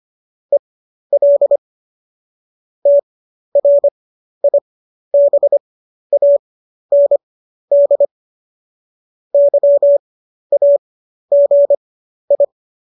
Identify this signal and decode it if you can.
Morse code, heard as a keyed tone, about 25 wpm, spaced as for 12 wpm, 570 Hz. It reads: EL TRIBAND YAGI
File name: qso012 12